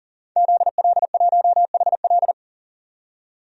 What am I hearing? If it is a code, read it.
Morse code, heard as a keyed tone, about 40 wpm, 700 Hz. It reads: ZL1HL
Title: DXCC-100 Call Signs - 5 Characters - 40wpm